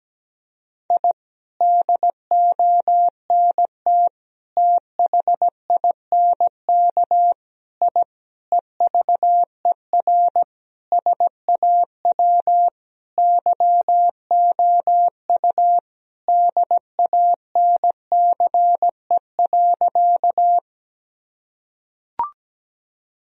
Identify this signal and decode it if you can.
Morse code, heard as a keyed tone, about 17 wpm, 700 Hz. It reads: I DONT THINK I EVER SAW YOU DANCE.
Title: Sentences from Top 1000 Words - Review Part 4 - 17wpm